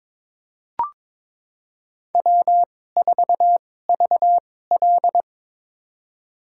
Morse code, heard as a keyed tone, about 22 wpm, spaced as for 16 wpm, 700 Hz. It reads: W4VL
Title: USA Call Signs - 4 Characters - 2x Character Spacing - 22wpm